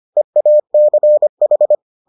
Morse code, heard as a keyed tone, about 25 wpm, 600 Hz. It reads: EACH